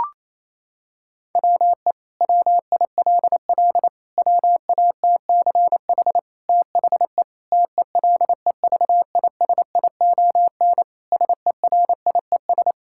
Morse code, heard as a keyed tone, about 28 wpm, 700 Hz. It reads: WE WILL WATCH THE TELEVISION SERIES
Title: Sentences from Top 1000 Words - Review Part 5 - 28wpm